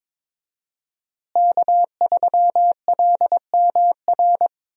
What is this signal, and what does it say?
Morse code, heard as a keyed tone, about 22 wpm, 700 Hz. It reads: K3LMR
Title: USA Call Signs - 5 Characters - Repeated 2x - 22wpm